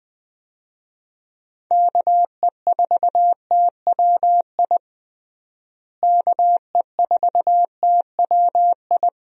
Morse code, heard as a keyed tone, about 20 wpm, 700 Hz. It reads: KE4TWI KE4TWI
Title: USA Call Signs - 6 Characters - Repeated 3x - 20wpm